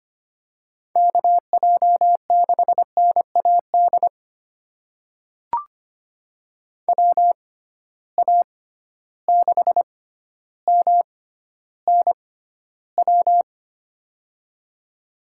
Morse code, heard as a keyed tone, about 25 wpm, spaced as for 13 wpm, 700 Hz. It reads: KJ6NAD WA6MNW
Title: USA Call Signs - 6 Characters - 6x Character Spacing - 25wpm